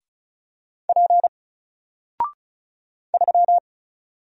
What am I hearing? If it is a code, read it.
Morse code, heard as a keyed tone, about 34 wpm, 700 Hz. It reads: P 3
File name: Single-Letter-Number-35wpm